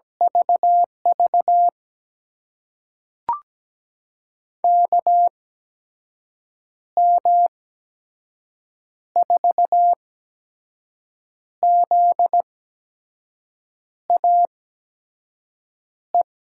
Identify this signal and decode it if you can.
Morse code, heard as a keyed tone, about 17 wpm, spaced as for 5 wpm, 700 Hz. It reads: VV KM4ZAE
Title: USA Call Signs - 8x Character Spacing - 17wpm